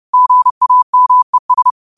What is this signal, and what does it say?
Morse code, heard as a keyed tone, about 30 wpm, 1000 Hz. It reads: GAMES
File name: qotdfiles / 30wpm